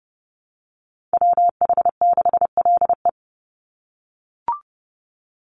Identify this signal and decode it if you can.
Morse code, heard as a keyed tone, about 30 wpm, 700 Hz. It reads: WH6LE
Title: NA SOTA Activator Callsigns 30wpm